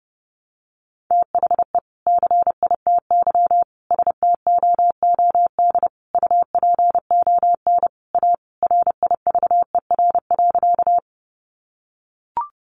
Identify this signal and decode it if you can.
Morse code, heard as a keyed tone, about 30 wpm, 700 Hz. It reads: THE CITY STOOD UPON A RIVER.
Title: Sentences from Top 1100 Words 30wpm